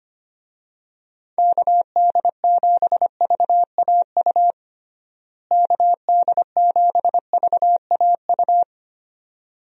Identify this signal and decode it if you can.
Morse code, heard as a keyed tone, about 25 wpm, 700 Hz. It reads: KD7VAU KD7VAU
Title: USA Call Signs - 6 Characters - Repeated 2x - 25wpm